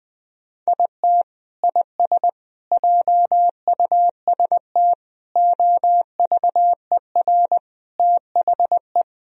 Morse code, heard as a keyed tone, about 21 wpm, 700 Hz. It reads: IT IS JUST OVER THE
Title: Sentences from Top 1000 Words - Review Part 1 - 20wpm